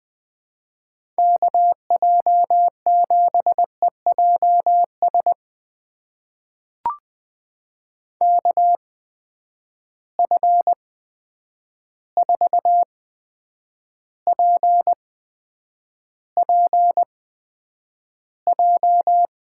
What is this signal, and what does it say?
Morse code, heard as a keyed tone, about 20 wpm, spaced as for 9 wpm, 700 Hz. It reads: KJ7EJS KF4PPJ
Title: USA Call Signs - 6 Characters - 8x Character Spacing - 20wpm